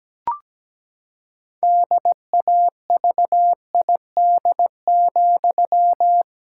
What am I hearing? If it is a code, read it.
Morse code, heard as a keyed tone, about 17 wpm, 700 Hz. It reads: DAVID,
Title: Pangram Challenge v2 - 17wpm